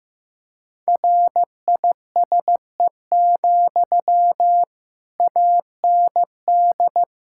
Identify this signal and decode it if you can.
Morse code, heard as a keyed tone, about 15 wpm, 700 Hz. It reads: RISE, AND
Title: Sentences from Top 1000 Words - Review Part 6 - 15wpm